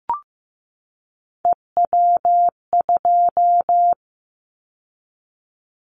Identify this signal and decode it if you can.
Morse code, heard as a keyed tone, about 15 wpm, 700 Hz. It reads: EW2
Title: Callsign Prefixes for Top 100 DXCC - 15wpm